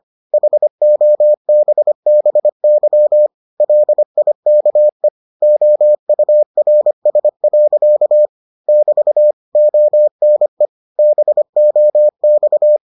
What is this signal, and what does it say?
Morse code, heard as a keyed tone, about 25 wpm, 600 Hz. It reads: HOBBY LIKE OURS. <BT> ONE BOX